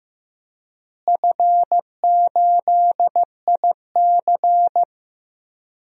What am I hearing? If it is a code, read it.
Morse code, heard as a keyed tone, about 15 wpm, 700 Hz. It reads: F8IC
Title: DXCC-100 Call Signs - 4 Characters - Repeated 2x - 15wpm